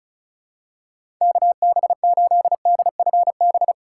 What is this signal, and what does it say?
Morse code, heard as a keyed tone, about 34 wpm, 700 Hz. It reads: KB8DFB
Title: USA Call Signs - 6 Characters - 2x Character Spacing - 35wpm